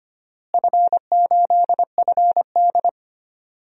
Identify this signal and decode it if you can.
Morse code, heard as a keyed tone, about 25 wpm, 700 Hz. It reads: F8FD
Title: DXCC-100 Call Signs - 4 Characters - Repeated 3x - 25wpm